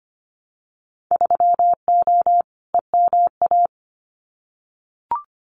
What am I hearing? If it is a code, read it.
Morse code, heard as a keyed tone, about 25 wpm, 700 Hz. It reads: 3O EMA
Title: Winter Field Day Exchanges Encore 25wpm